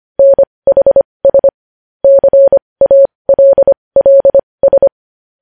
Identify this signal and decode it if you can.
Morse code, heard as a keyed tone, about 25 wpm, spaced as for 20 wpm, 550 Hz. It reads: NHS CALLS
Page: news-headlines-in-morse-code-at-20-wpm